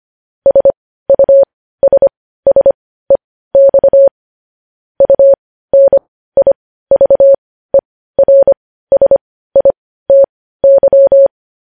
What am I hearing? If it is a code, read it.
Morse code, heard as a keyed tone, about 25 wpm, spaced as for 15 wpm, 550 Hz. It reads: SUSSEX UNIVERSITY
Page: news-headlines-in-morse-code-at-15-wpm